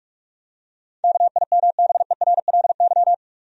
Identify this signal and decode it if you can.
Morse code, heard as a keyed tone, about 46 wpm, 700 Hz. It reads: KIMBERLY
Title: USA Common Names - 45wpm